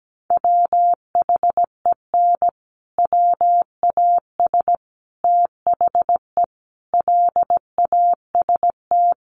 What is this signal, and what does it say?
Morse code, heard as a keyed tone, about 17 wpm, 700 Hz. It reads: WHEN WAS THE LAST
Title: Sets of 5 Words Encore 17wpm